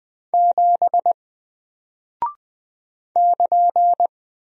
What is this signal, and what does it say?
Morse code, heard as a keyed tone, about 20 wpm, 700 Hz. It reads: 7 <KN>
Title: Single Character (Letter:Number:Punctuation:Prosign) Alt 20wpm